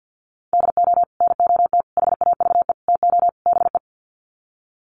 Morse code, heard as a keyed tone, about 48 wpm, 700 Hz. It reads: DO NOT HAVE TO BE
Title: Sets of 5 Words 50wpm